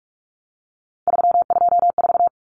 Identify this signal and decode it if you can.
Morse code, heard as a keyed tone, about 44 wpm, 700 Hz. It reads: 324
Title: Numbers - 3-Digit 45wpm